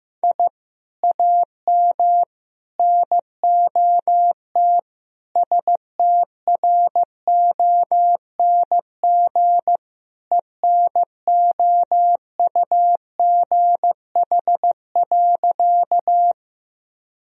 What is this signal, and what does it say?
Morse code, heard as a keyed tone, about 15 wpm, 700 Hz. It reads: I AM NOT STRONG ENOUGH.
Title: Sentences from Top 1000 Words - Review Part 3 - 15wpm